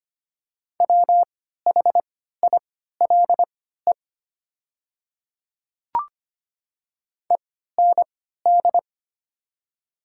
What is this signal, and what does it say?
Morse code, heard as a keyed tone, about 25 wpm, spaced as for 14 wpm, 700 Hz. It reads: WHILE END
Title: 3x Character Spacing - Top 300 Words - 25wpm